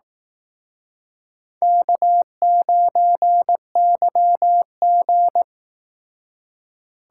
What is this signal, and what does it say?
Morse code, heard as a keyed tone, about 18 wpm, 700 Hz. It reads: K9YG
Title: Speed Racing - USA 1x2 Call Signs 15wpm